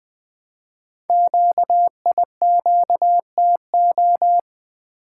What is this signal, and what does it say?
Morse code, heard as a keyed tone, about 20 wpm, 700 Hz. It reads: QIQTO
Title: Code Groups - 5 Letters - 20wpm